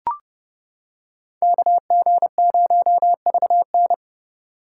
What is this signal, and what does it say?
Morse code, heard as a keyed tone, about 29 wpm, 700 Hz. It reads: KG0VN